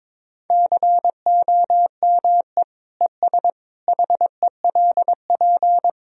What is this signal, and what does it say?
Morse code, heard as a keyed tone, about 22 wpm, 700 Hz. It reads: COME ES HELP
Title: Sentences from Top 200 Words and Abbreviations 22wpm